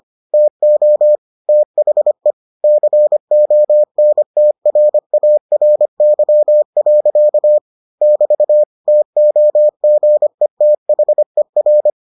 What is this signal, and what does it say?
Morse code, heard as a keyed tone, about 25 wpm, 600 Hz. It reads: TO THE CONTRARY. <BT> TOGETHER